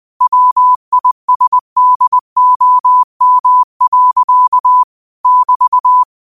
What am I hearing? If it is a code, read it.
Morse code, heard as a keyed tone, about 20 wpm, 1000 Hz. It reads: WISDOM. <BT>
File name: qotdfiles / 20wpm